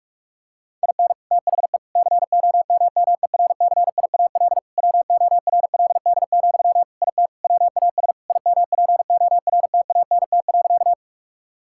Morse code, heard as a keyed tone, about 46 wpm, 700 Hz. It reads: IN THE COMMERCIAL WORLD, IT WAS IMPORTANT.
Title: Sentences from Top 1000 Words - Review Part 5 - 45wpm